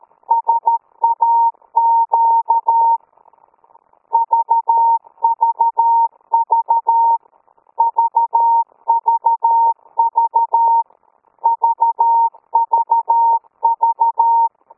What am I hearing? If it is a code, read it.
Morse code, heard as a keyed tone, about 13 wpm, 920 Hz. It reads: SAQ VVV VVV VVV